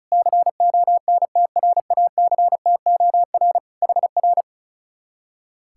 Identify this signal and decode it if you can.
Morse code, heard as a keyed tone, about 41 wpm, 700 Hz. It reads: CONTRACTOR HR
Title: Speed-Racing Common QSO Phrases - 35wpm